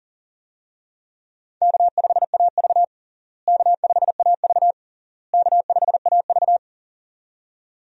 Morse code, heard as a keyed tone, about 40 wpm, 700 Hz. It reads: K5AV K5AV K5AV
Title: USA Call Signs - 4 Characters - Repeated 3x - 40wpm